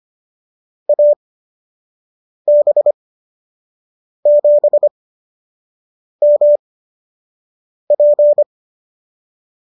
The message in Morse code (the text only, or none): AB7MP